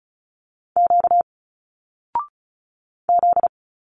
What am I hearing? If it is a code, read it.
Morse code, heard as a keyed tone, about 35 wpm, 700 Hz. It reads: Q Z
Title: Single Letters 35wpm